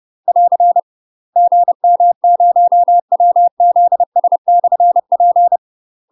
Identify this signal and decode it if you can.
Morse code, heard as a keyed tone, about 30 wpm, 700 Hz. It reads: <AR> GM0WZS/P
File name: UK MASTER QSOs TRAINING / 30WPM / UK MASTER QSOs 0040